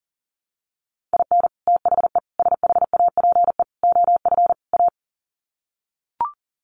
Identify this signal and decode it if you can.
Morse code, heard as a keyed tone, about 37 wpm, 700 Hz. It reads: IN THE SHAPE OF A
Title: Sets of 5 Words Encore 40wpm